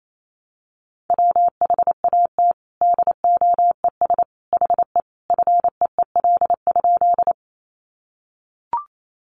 Morse code, heard as a keyed tone, about 28 wpm, 700 Hz. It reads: WHAT DOES HE FEEL?
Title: Sentences from Top 400 Words 28wpm